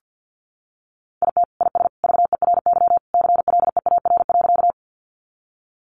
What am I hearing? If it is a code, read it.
Morse code, heard as a keyed tone, about 50 wpm, 700 Hz. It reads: IT IS VERY CLEAR.
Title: Sentences from Top 500 Words 50wpm